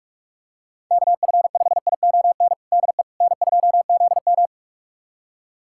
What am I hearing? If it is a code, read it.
Morse code, heard as a keyed tone, about 42 wpm, 700 Hz. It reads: KF5ION DE N2ZM
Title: USA Callsign DE Callsign 45wpm